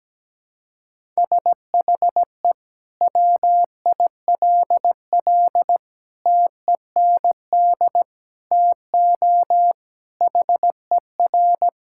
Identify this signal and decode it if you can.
Morse code, heard as a keyed tone, about 17 wpm, 700 Hz. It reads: SHE WILL TEND TO HER
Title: Sentences from Top 1000 Words - Review Part 5 - 17wpm